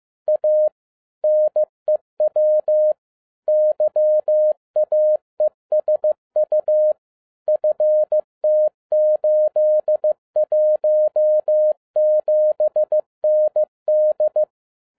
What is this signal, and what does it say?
Morse code, heard as a keyed tone, about 15 wpm, 610 Hz. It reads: A NEW YAESU FT817ND
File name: MY CW LIFE 1 15wpm